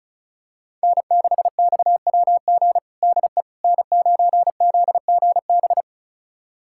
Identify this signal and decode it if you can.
Morse code, heard as a keyed tone, about 36 wpm, 700 Hz. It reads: N6XWG DE N9ZGB
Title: USA Callsign DE Callsign - 5 Characters - 35wpm